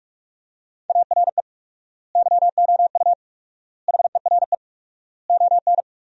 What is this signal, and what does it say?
Morse code, heard as a keyed tone, about 46 wpm, 700 Hz. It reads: ARE YOU HERE ON
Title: 4x Word-Spacing - Sentences from Top 400 Words - 45wpm